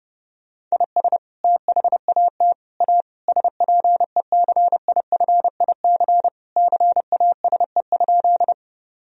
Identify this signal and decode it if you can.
Morse code, heard as a keyed tone, about 31 wpm, 700 Hz. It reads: IS THAT A SPECIFIC CASE?
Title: Sentences from Top 1000 Words - Review Part 3 - 30wpm